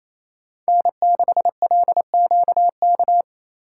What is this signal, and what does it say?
Morse code, heard as a keyed tone, about 28 wpm, 700 Hz. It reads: N6LQK
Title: USA Call Signs - 5 Characters - 6x Character Spacing - 28wpm